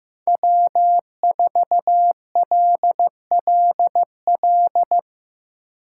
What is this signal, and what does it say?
Morse code, heard as a keyed tone, about 15 wpm, 700 Hz. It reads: W4LLL
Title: USA Call Signs - 5 Characters - 4x Character Spacing - 15wpm